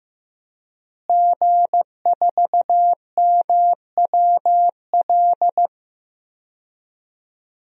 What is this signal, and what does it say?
Morse code, heard as a keyed tone, about 15 wpm, 700 Hz. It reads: G4MWL
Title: DXCC-100 Call Signs - 5 Characters - Repeated 2x - 15wpm